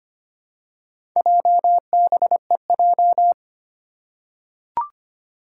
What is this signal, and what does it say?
Morse code, heard as a keyed tone, about 25 wpm, 700 Hz. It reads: JBEJ
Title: Four Challenging Letter-Number Combinations 25wpm